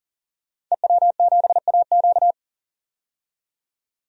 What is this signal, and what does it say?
Morse code, heard as a keyed tone, about 40 wpm, 700 Hz. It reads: EW7AQ
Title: DXCC-100 Call Signs - 5 Characters - 40wpm